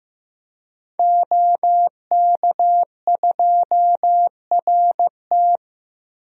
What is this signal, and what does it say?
Morse code, heard as a keyed tone, about 15 wpm, 700 Hz. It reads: OK2RT
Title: DXCC-100 Call Signs - 5 Characters - Repeated 3x - 15wpm